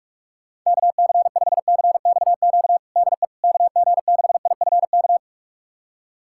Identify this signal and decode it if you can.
Morse code, heard as a keyed tone, about 46 wpm, 700 Hz. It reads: KC5CXQ DE KG6IFK
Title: USA Callsign DE Callsign 45wpm